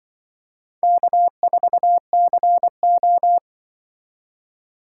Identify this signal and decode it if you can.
Morse code, heard as a keyed tone, about 26 wpm, 700 Hz. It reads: K4CO